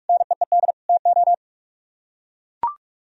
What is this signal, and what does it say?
Morse code, heard as a keyed tone, about 42 wpm, 700 Hz. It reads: NEED TO